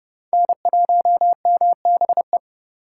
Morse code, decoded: N1MBE

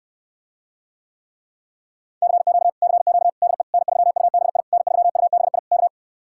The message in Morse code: CQ CQ DE N3UBI N3UBI K

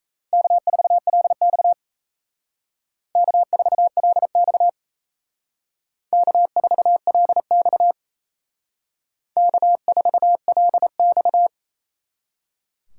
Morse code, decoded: K4LX K4LX K4LX K4LX

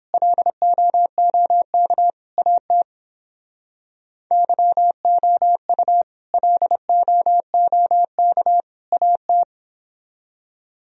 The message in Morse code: LOOK AT YOU LOOK AT

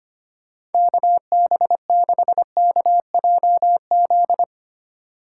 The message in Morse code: KB6KJZ

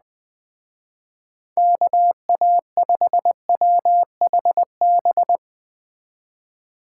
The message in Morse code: KA5WHB

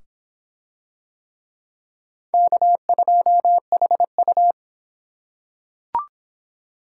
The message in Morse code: K2HU

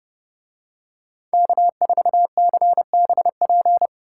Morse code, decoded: K4CBP

